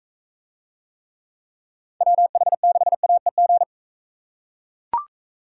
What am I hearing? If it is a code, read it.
Morse code, heard as a keyed tone, about 41 wpm, 700 Hz. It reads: WH6AEG